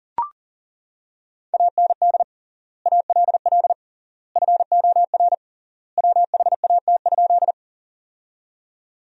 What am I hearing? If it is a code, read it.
Morse code, heard as a keyed tone, about 40 wpm, 700 Hz. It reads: AND ALL FOR WHAT?
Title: 3x Word-Spacing - Sentences from Top 100 Words - 40wpm